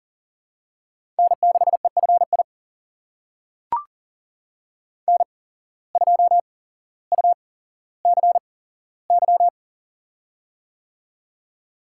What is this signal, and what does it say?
Morse code, heard as a keyed tone, about 37 wpm, spaced as for 17 wpm, 700 Hz. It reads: N6EFI N2UCY
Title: USA Call Signs - 5 Characters - 8x Character Spacing - 40wpm